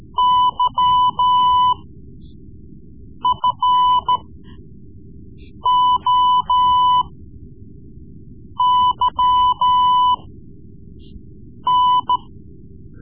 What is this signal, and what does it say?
Morse code, heard as a keyed tone, about 13 wpm, spaced as for 5 wpm, 1000 Hz. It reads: YFOYN
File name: Morse Lessons / Morse Lessons / Tape-3B